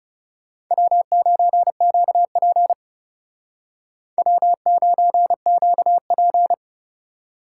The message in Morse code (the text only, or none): W9QP W9QP